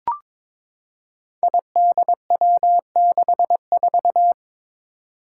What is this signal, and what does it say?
Morse code, heard as a keyed tone, about 22 wpm, 700 Hz. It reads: IDW64
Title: Five Challenging Letter-Number Combinations 22wpm